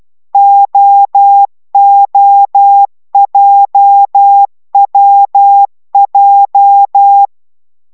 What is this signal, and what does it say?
Morse code, heard as a keyed tone, about 12 wpm, 800 Hz. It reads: OOJWJ